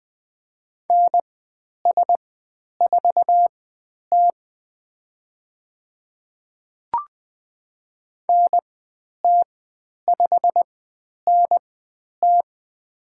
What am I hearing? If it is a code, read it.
Morse code, heard as a keyed tone, about 20 wpm, spaced as for 10 wpm, 700 Hz. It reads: NS4T NT5NT
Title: 109 Lesson 18g - Letter m - Call Signs 20wpm